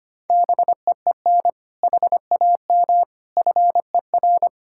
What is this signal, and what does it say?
Morse code, heard as a keyed tone, about 25 wpm, 700 Hz. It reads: BEEN HAM FER
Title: Practice QSO - Exchange 3 - More Info - 25wpm